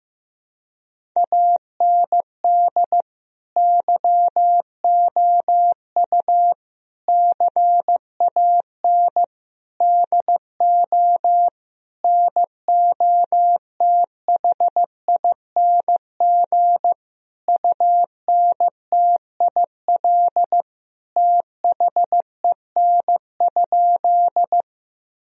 Morse code, heard as a keyed tone, about 15 wpm, 700 Hz. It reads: AND YOU CAN DO NOTHING UNTIL THEN?
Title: Sentences from Top 1000 Words - Review Part 1 - 15wpm